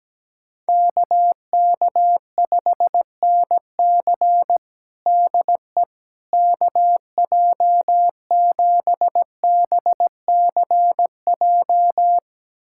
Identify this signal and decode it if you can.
Morse code, heard as a keyed tone, about 17 wpm, 700 Hz. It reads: KK5NC DE KJ7BCJ
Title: USA Callsign DE Callsign 17wpm